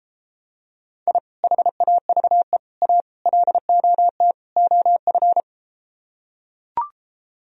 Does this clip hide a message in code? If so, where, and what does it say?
Yes: Morse code from 1.1 to 5.4 s, heard as a keyed tone, about 34 wpm, 700 Hz. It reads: I HAVE A LOT OF